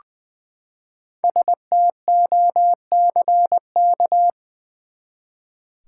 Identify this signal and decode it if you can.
Morse code, heard as a keyed tone, about 20 wpm, 700 Hz. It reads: STOCK
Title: Five Letter Words - 20wpm